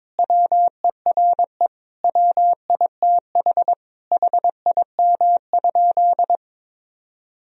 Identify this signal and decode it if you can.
Morse code, heard as a keyed tone, about 22 wpm, 700 Hz. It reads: WERE WITH HIM?
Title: Sentences from Top 1000 Words - Review Part 1 - 22wpm